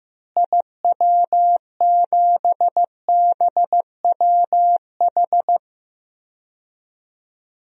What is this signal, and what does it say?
Morse code, heard as a keyed tone, about 15 wpm, 700 Hz. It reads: IW7BWH
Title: DXCC-100 Call Signs - 6 Characters - Repeated 2x - 15wpm